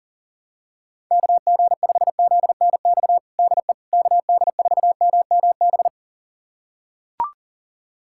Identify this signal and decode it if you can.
Morse code, heard as a keyed tone, about 40 wpm, 700 Hz. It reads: KG5ZNX DE KD4MMB
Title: USA Callsign DE Callsign 40wpm